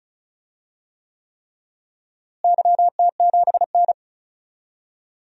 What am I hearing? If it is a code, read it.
Morse code, heard as a keyed tone, about 36 wpm, 700 Hz. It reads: YT7N
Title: DXCC-100 Call Signs - 4 Characters - Repeated 3x - 35wpm